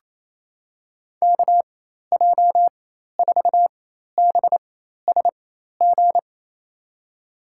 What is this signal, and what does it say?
Morse code, heard as a keyed tone, about 28 wpm, spaced as for 13 wpm, 700 Hz. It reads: KJ4BSG
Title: USA Call Signs - 4x Character Spacing - 28wpm